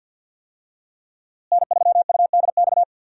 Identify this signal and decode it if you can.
Morse code, heard as a keyed tone, about 46 wpm, 690 Hz. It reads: N3UDX
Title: USA Call Signs - 5 Characters - Repeated 3x - 50wpm